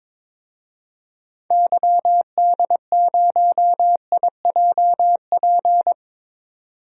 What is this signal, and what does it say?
Morse code, heard as a keyed tone, about 22 wpm, 700 Hz. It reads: YD0IJP
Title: DXCC-100 Call Signs - 6 Characters - 22wpm